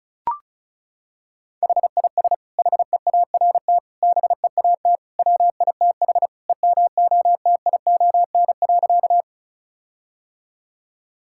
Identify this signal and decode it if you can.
Morse code, heard as a keyed tone, about 36 wpm, 700 Hz. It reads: HIS HEART BEAT WITH EMOTION.
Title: Sentences from Top 1600 Words 35wpm